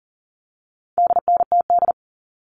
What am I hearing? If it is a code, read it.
Morse code, heard as a keyed tone, about 40 wpm, 700 Hz. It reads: DNTD
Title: All Cut Numbers - 4-Digit 40wpm